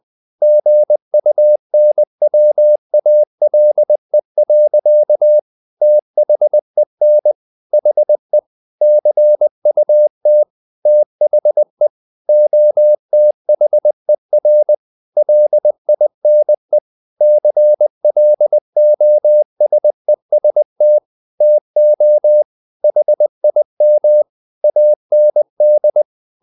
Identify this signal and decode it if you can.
Morse code, heard as a keyed tone, about 20 wpm, 600 Hz. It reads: GUNWALE. THEN HE CUT THE OTHER LINE CLOSEST TO HIM AND
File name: the old man and the sea / the old man and the sea part 1 20wpm